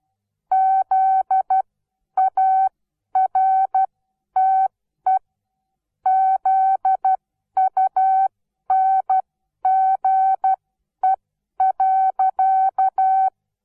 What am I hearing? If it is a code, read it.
Morse code, heard as a keyed tone, about 12 wpm, spaced as for 10 wpm, 760 Hz. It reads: ZARTE ZUNGE.
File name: Onlinelehrgang / morsen / DJ4UF Morsen17